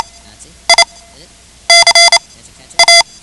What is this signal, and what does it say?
Morse code, heard as a keyed tone, about 28 wpm, spaced as for 11 wpm, 2050 Hz. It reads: I CA
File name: ham radio / CodeQuick / Code Quick 10